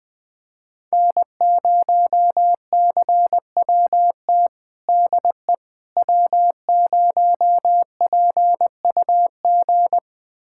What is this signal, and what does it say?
Morse code, heard as a keyed tone, about 20 wpm, 700 Hz. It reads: N0CWT DE W0PUG
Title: USA Callsign DE Callsign - 5 Characters - 20wpm